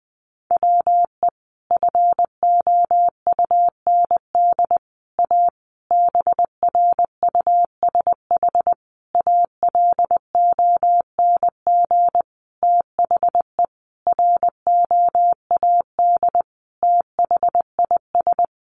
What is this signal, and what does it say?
Morse code, heard as a keyed tone, about 20 wpm, 700 Hz. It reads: WE FOUND A BRUSH ALONG THE ROAD THIS
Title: Sentences from Top 1100 Words 20wpm